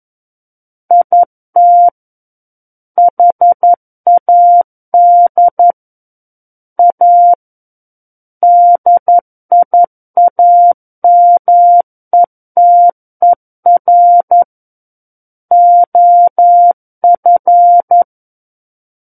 Morse code, heard as a keyed tone, about 11 wpm, 700 Hz. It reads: IT HAD A DIAMETER OF
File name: Book One / 02-Chapter 2 - 11 WPM